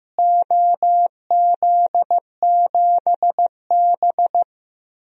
Morse code, heard as a keyed tone, about 15 wpm, 700 Hz. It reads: OZ7B
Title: DXCC-100 Call Signs - 4 Characters - Repeated 3x - 15wpm